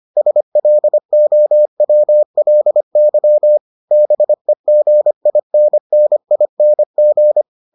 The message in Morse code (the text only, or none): SLOWLY BEGINNING